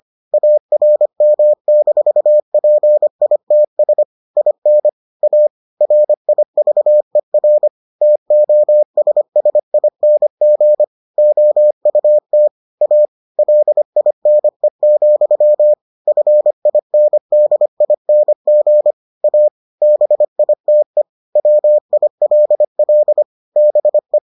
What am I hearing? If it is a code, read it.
Morse code, heard as a keyed tone, about 25 wpm, 600 Hz. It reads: ARM-PITS IN A RIVER TOSSING OUT A LINE, FINDING A BITE WILL BE